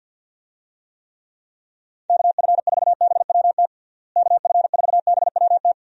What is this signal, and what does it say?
Morse code, heard as a keyed tone, about 51 wpm, 700 Hz. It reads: KF4BWT KF4BWT
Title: USA Call Signs - 6 Characters - Repeated 3x - 50wpm